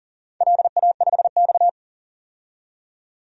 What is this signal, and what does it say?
Morse code, heard as a keyed tone, about 44 wpm, 690 Hz. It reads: LA5X
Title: DXCC-100 Call Signs - 4 Characters - 40wpm